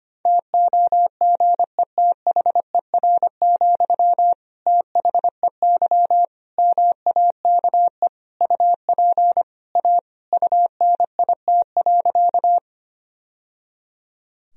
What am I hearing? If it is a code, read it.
Morse code, heard as a keyed tone, about 25 wpm, 700 Hz. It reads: TOGETHER, THEY MAKE UP A UNIT.
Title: Sentences from Top 1000 Words - Review Part 2 - 25wpm